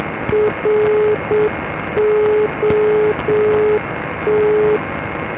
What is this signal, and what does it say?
Morse code, heard as a keyed tone, about 7 wpm, 420 Hz, with noise. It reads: ROT
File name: digital-modes / ndb